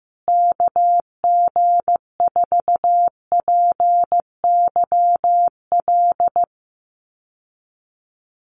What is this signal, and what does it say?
Morse code, heard as a keyed tone, about 15 wpm, 700 Hz. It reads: KG4PYL
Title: USA Call Signs Repeated 2x 15wpm